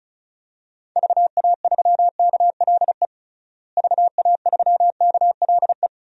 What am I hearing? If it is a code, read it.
Morse code, heard as a keyed tone, about 36 wpm, 700 Hz. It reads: VA3KLE VA3KLE
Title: DXCC-100 Call Signs - 6 Characters - Repeated 3x - 35wpm